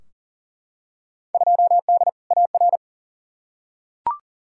Morse code, heard as a keyed tone, about 40 wpm, 700 Hz. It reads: 2D AR
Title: Field Day Exchanges Encore - 40wpm